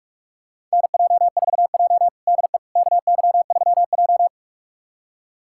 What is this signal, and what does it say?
Morse code, heard as a keyed tone, about 45 wpm, 700 Hz. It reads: NJ4J DE KY3J
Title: USA Callsign DE Callsign - 4 Characters - 45wpm